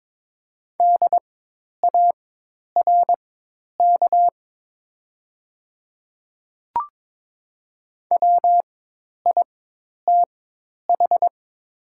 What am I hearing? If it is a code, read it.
Morse code, heard as a keyed tone, about 22 wpm, spaced as for 10 wpm, 700 Hz. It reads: DARK WITH
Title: 4x Character Spacing - Top 500 Words - 22wpm